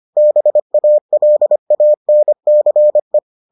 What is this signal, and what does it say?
Morse code, heard as a keyed tone, about 25 wpm, 600 Hz. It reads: BALANCE